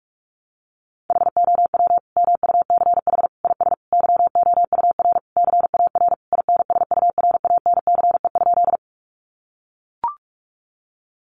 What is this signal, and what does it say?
Morse code, heard as a keyed tone, about 44 wpm, 700 Hz. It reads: HOW MUCH IS YOUR CAR INSURANCE?
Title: Sentences from Top 1100 Words 45wpm